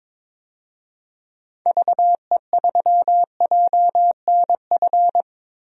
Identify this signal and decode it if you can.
Morse code, heard as a keyed tone, about 22 wpm, 700 Hz. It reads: VE3JNF
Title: DXCC-100 Call Signs - 6 Characters - 22wpm